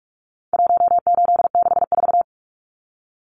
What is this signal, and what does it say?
Morse code, heard as a keyed tone, about 44 wpm, 700 Hz. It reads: 1864
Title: Numbers - 4-Digit 45wpm